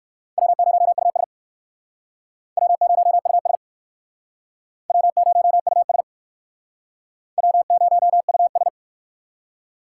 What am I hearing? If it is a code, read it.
Morse code, heard as a keyed tone, about 60 wpm, 700 Hz. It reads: W0US W0US W0US W0US